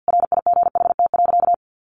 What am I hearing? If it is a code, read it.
Morse code, heard as a keyed tone, about 50 wpm, 700 Hz. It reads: RIGHT.
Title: Sentences from Top 200 Words 50wpm